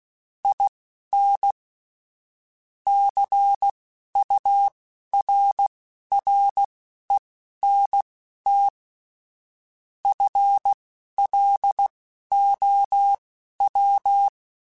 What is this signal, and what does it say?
Morse code, heard as a keyed tone, about 16 wpm, spaced as for 11 wpm, 800 Hz. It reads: IN CURRENT FLOW
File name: Technical Subjects / Technical Subjects-mp3s / Reactance @10